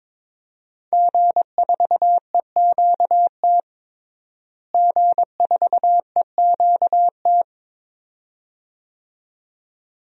Morse code, heard as a keyed tone, about 22 wpm, 700 Hz. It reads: G4EQT G4EQT
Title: DXCC-100 Call Signs - 5 Characters - Repeated 2x - 22wpm